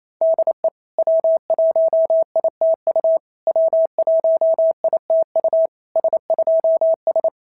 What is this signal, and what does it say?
Morse code, heard as a keyed tone, about 27 wpm, 650 Hz. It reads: DE W1ITU W1ITU S2S
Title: SOTA QSO Practice 28wpm